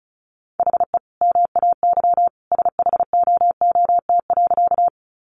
Morse code, heard as a keyed tone, about 34 wpm, 700 Hz. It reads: HE MAY SHOOT.
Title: Sentences from Top 1200 Words 35wpm